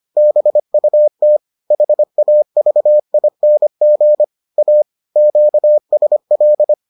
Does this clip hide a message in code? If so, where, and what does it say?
Yes: Morse code from 0.2 to 6.7 s, heard as a keyed tone, about 25 wpm, 600 Hz. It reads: BUT HAVING A QSL